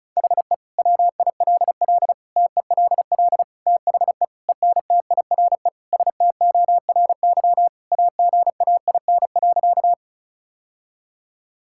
Morse code, heard as a keyed tone, about 36 wpm, 700 Hz. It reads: HE WILL TELL THE ENTIRE STORY AGAIN.
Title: Sentences from Top 1000 Words - Review Part 5 - 35wpm